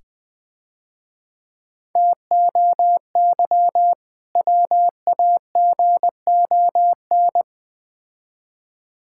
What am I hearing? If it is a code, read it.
Morse code, heard as a keyed tone, about 20 wpm, 690 Hz. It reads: TOY WAGON
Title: Difficult Words - Speed-Racing with Hints - 20wpm